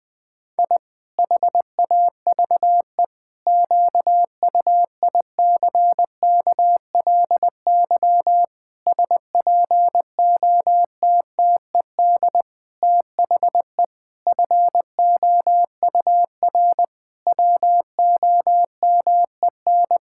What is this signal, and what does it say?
Morse code, heard as a keyed tone, about 20 wpm, 700 Hz. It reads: I HAVE QUICKLY SPOTTED THE FOUR WOMEN
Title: Pangram Challenge v2 - 20wpm